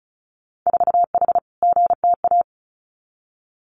Morse code, heard as a keyed tone, about 36 wpm, 700 Hz. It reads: 4H GTA
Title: Winter Field Day Exchanges Encore 35wpm